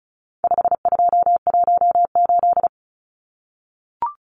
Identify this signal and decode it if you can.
Morse code, heard as a keyed tone, about 34 wpm, 700 Hz. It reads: 5218
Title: Numbers - 4-Digit 35wpm